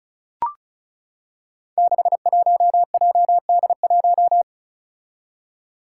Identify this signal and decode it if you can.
Morse code, heard as a keyed tone, about 36 wpm, 700 Hz. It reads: 61JD1
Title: Five Challenging Letter-Number Combinations 35wpm